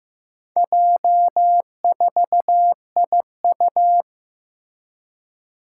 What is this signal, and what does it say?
Morse code, heard as a keyed tone, about 15 wpm, 700 Hz. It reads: J4IU
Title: Four Challenging Letter-Number Combinations 15wpm